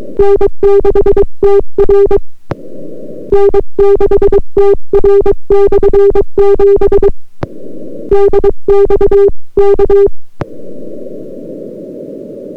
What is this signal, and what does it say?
Morse code, heard as a keyed tone, about 22 wpm, 400 Hz. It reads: N6TR N6TR/7 DX K